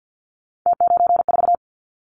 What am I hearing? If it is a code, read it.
Morse code, heard as a keyed tone, about 50 wpm, 700 Hz. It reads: T94
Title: Callsign Prefixes for Top 50 DXCC - 50wpm